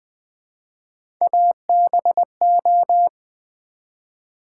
Morse code, heard as a keyed tone, about 20 wpm, 700 Hz. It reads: ABO
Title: Build A Word - 1 Letter At A Time - 5-Letter Words - 20wpm